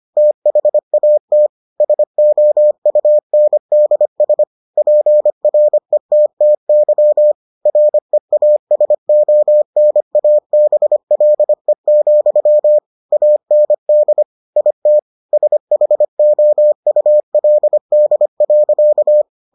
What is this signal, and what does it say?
Morse code, heard as a keyed tone, about 25 wpm, 600 Hz. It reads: THAT SOUNDS PRETTY REASONABLE, AND IT SHOULD.